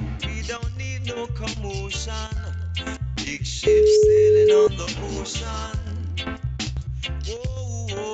Ringback tone (Europe, 425 Hz).